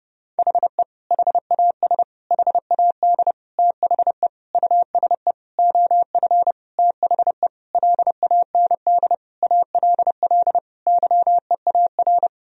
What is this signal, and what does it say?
Morse code, heard as a keyed tone, about 30 wpm, 700 Hz. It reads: HE HAS HAD THE USE OF THE LAND ALL YEAR
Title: Sentences from Top 1000 Words - Review Part 1 - 30wpm